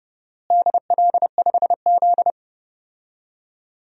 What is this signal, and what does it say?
Morse code, heard as a keyed tone, about 30 wpm, 700 Hz. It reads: DL5Z
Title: DXCC-100 Call Signs - 4 Characters - Repeated 2x - 30wpm